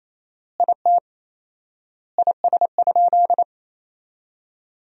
Morse code, heard as a keyed tone, about 28 wpm, 700 Hz. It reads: IT IS?
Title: 4x Word-Spacing - Sentences from Top 100 Words - 28wpm